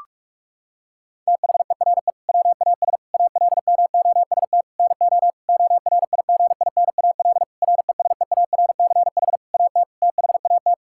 Morse code, heard as a keyed tone, about 46 wpm, 700 Hz. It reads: THERE WAS ALMOST NO ORIGINAL RESEARCH AT THAT
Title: Sentences from Top 1000 Words - Review Part 6 - 45wpm